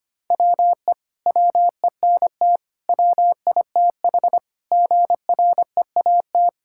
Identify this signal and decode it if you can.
Morse code, heard as a keyed tone, about 25 wpm, 700 Hz. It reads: WE WENT WITH GREAT
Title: Sentences from Top 1000 Words - Review Part 5 - 25wpm